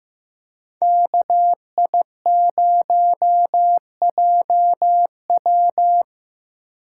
Morse code, heard as a keyed tone, about 17 wpm, 700 Hz. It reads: KI0JW
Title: Speed Racing - USA Call Signs 15wpm